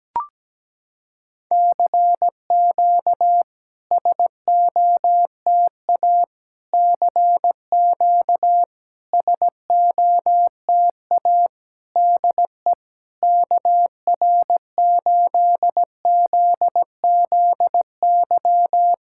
Morse code, heard as a keyed tone, about 17 wpm, 700 Hz. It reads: CQ SOTA CQ SOTA DE KR8ZZY
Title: SOTA QSO Practice 17wpm